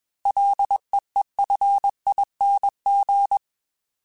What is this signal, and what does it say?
Morse code, heard as a keyed tone, about 21 wpm, 800 Hz. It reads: LEEFING
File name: Ls / Single L m3s / Single L Prefixe 3x2 Twice @20